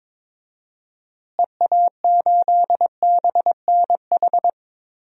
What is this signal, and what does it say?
Morse code, heard as a keyed tone, about 22 wpm, 700 Hz. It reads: EA8BNH
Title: DXCC-100 Call Signs - 6 Characters - Repeated 3x - 22wpm